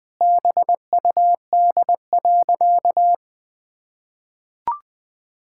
Morse code, heard as a keyed tone, about 20 wpm, 700 Hz. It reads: BUD.
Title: Pangram Challenge v2 - 20wpm